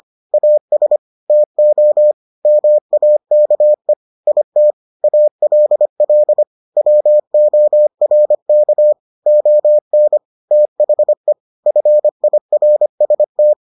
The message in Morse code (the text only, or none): AS TO MAKE IT ALL WORK ON THE FIRST